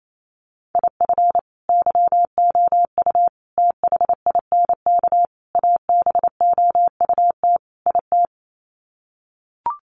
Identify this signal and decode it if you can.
Morse code, heard as a keyed tone, about 28 wpm, 700 Hz. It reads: IF YOU THINK ABOUT IT
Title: Sets of 5 Words Encore2 28wpm